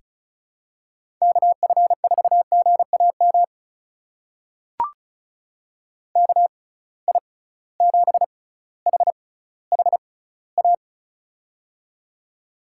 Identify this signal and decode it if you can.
Morse code, heard as a keyed tone, about 34 wpm, spaced as for 18 wpm, 700 Hz. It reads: KF4GAM KI7HHA
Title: USA Call Signs - 6 Characters - 6x Character Spacing - 35wpm